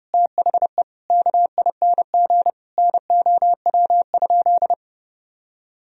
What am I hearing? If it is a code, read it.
Morse code, heard as a keyed tone, about 30 wpm, 700 Hz. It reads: THE KING NOW?
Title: Sentences from Top 1000 Words - Review Part 4 - 30wpm